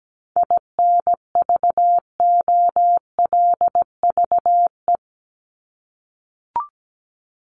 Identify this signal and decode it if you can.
Morse code, heard as a keyed tone, about 17 wpm, 700 Hz. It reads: INVOLVE
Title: Top 1001-1250 Words 17wpm